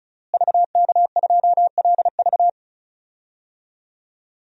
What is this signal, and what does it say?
Morse code, heard as a keyed tone, about 35 wpm, 700 Hz. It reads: VK2LV
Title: DXCC-100 Call Signs - 5 Characters - Repeated 2x - 35wpm